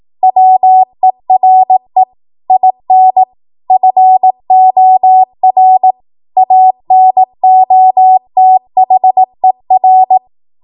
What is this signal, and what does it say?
Morse code, heard as a keyed tone, about 18 wpm, 750 Hz. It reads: WERE IN FOR ANOTHER
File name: Morse / 18WPM